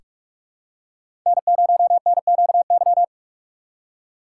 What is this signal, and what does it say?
Morse code, heard as a keyed tone, about 43 wpm, 700 Hz. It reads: N0NQY
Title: USA Call Signs - 2x Character Spacing - 45wpm